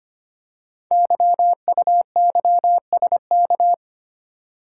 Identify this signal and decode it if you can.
Morse code, heard as a keyed tone, about 25 wpm, 700 Hz. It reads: YUYSK
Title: Code Groups - 5 Letters - 25wpm